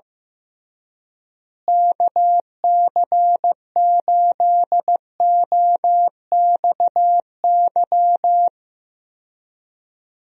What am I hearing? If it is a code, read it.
Morse code, heard as a keyed tone, about 16 wpm, 700 Hz. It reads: KC8OXY
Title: USA Call Signs - 6 Characters - Repeated 3x - 15wpm